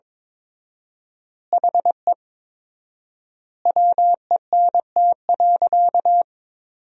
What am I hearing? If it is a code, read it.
Morse code, heard as a keyed tone, about 22 wpm, 700 Hz. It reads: HE WENT.
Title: 4x Word-Spacing - Sentences from Top 200 Words - 22wpm